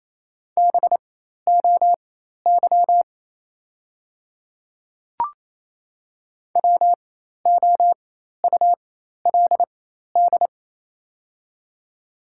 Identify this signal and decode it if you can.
Morse code, heard as a keyed tone, about 28 wpm, spaced as for 13 wpm, 700 Hz. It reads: BOY WOULD
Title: 4x Character Spacing - Top 300 Words - 28wpm